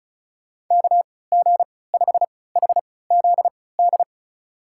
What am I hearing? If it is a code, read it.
Morse code, heard as a keyed tone, about 36 wpm, spaced as for 20 wpm, 700 Hz. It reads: KG5HZD
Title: USA Call Signs - 6 Characters - 3x Character Spacing - 35wpm